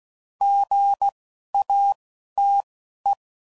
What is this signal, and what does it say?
Morse code, heard as a keyed tone, about 16 wpm, spaced as for 11 wpm, 800 Hz. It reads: GATE
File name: CW Code Groups @10WPM/Technical Subjects